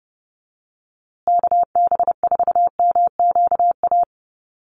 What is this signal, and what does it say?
Morse code, heard as a keyed tone, about 30 wpm, 700 Hz. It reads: KB4MQA
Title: USA Call Signs Repeated 3x 30wpm